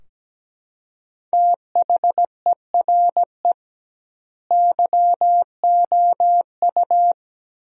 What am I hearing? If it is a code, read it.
Morse code, heard as a keyed tone, about 17 wpm, 700 Hz. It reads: THERE YOU
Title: 2x Word-Spacing - Sentences from Top 100 Words - 17wpm